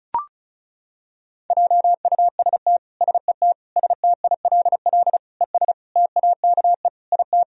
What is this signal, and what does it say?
Morse code, heard as a keyed tone, about 36 wpm, 700 Hz. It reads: JUST SET STILL ES TAKE IT
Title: Sentences from Top 300 Words and Abbreviations 35wpm